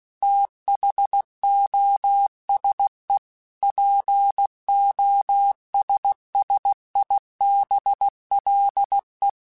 Morse code, heard as a keyed tone, about 16 wpm, 800 Hz. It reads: THOSE POSSIBLE
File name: Technical Subjects / Technical Subjects-mp3s / AND Gate @15